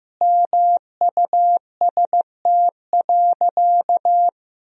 Morse code, heard as a keyed tone, about 15 wpm, 700 Hz. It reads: MUST.
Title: Sentences from Top 200 Words and Abbreviations 15wpm